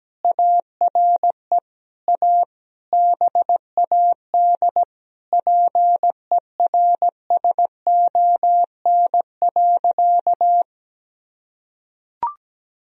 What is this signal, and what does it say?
Morse code, heard as a keyed tone, about 17 wpm, 700 Hz. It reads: ARE A BAD PERSON.
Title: Sentences from Top 1000 Words - Review Part 3 - 17wpm